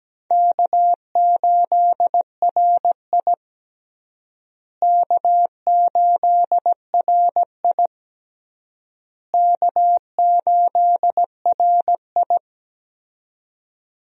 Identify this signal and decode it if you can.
Morse code, heard as a keyed tone, about 17 wpm, 700 Hz. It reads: K8RI K8RI K8RI
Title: USA Call Signs - 4 Characters - Repeated 3x - 17wpm